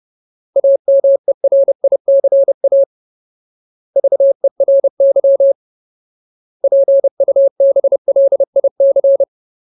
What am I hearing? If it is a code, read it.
Morse code, heard as a keyed tone, about 30 wpm, 550 Hz. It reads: AMERICA VERY PUBLIC
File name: Ultimate Word List / 30WPM Ultimate